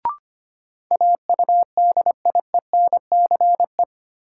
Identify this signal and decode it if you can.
Morse code, heard as a keyed tone, about 25 wpm, 700 Hz. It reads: AUDIENCE
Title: Sentences from Top 800 Words - New Word Drill - 25wpm